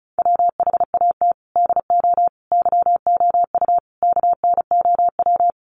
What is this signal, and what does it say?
Morse code, heard as a keyed tone, about 34 wpm, 700 Hz. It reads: WHAT DO YOU KNOW
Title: Sentences from Top 100 Words 35wpm